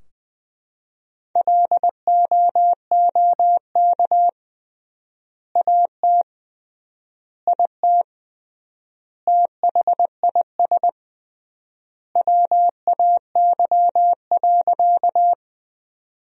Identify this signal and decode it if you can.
Morse code, heard as a keyed tone, about 20 wpm, 700 Hz. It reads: LOOK AT IT THIS WAY.
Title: 3x Word-Spacing - Sentences from Top 100 Words - 20wpm